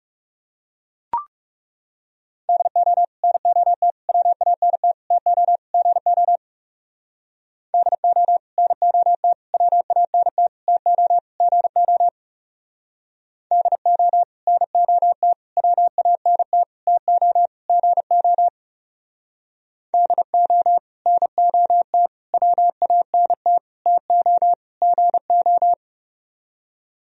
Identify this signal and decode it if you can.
Morse code, heard as a keyed tone, about 37 wpm, 700 Hz. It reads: DO NOT WANT TO GO DO NOT WANT TO GO DO NOT WANT TO GO DO NOT WANT TO GO
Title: Speed-Racing - Sets of 5 Words 30wpm